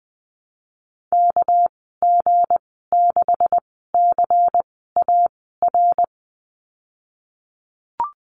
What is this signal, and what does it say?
Morse code, heard as a keyed tone, about 20 wpm, spaced as for 14 wpm, 700 Hz. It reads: KG6CAR
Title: 213 Lesson 34g - Number 8 - Call Signs 20wpm